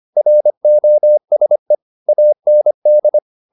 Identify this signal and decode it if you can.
Morse code, heard as a keyed tone, about 25 wpm, 600 Hz. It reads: ROSE AND
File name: the old man and the sea part 1 25wpm